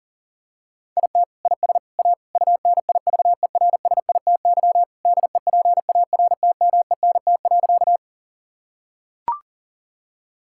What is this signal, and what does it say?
Morse code, heard as a keyed tone, about 40 wpm, 700 Hz. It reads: IT IS A UNIVERSITY DEPARTMENT.
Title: Sentences from Top 1000 Words - Review Part 5 - 40wpm